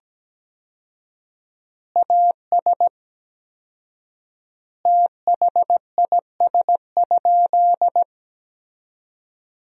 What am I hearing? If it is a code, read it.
Morse code, heard as a keyed tone, about 17 wpm, 700 Hz. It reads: AS THIS?
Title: 4x Word-Spacing - Sentences from Top 400 Words - 17wpm